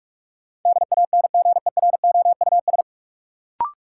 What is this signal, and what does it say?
Morse code, heard as a keyed tone, about 46 wpm, 700 Hz. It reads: DANGEROUS
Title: Sentences from Top 1200 Words - New Word Drill Alt - 45wpm